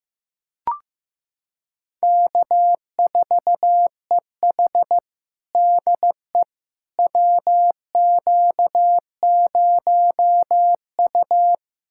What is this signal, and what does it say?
Morse code, heard as a keyed tone, about 15 wpm, 700 Hz. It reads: K4EH DE WQ0U
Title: USA Callsign DE Callsign - 4 Characters - 15wpm